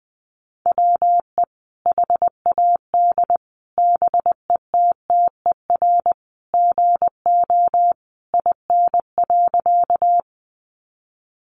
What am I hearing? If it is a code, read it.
Morse code, heard as a keyed tone, about 20 wpm, 700 Hz. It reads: WE HAD BETTER GO IN.
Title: Sentences from Top 400 Words 20wpm